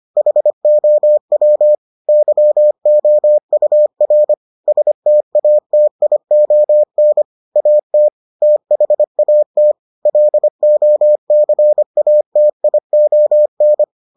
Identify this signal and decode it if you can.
Morse code, heard as a keyed tone, about 25 wpm, 600 Hz. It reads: HOW YOUR STATION AT THAT LOCATION